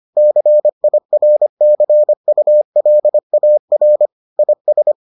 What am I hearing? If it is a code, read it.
Morse code, heard as a keyed tone, about 25 wpm, 600 Hz. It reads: CIRCULAR IS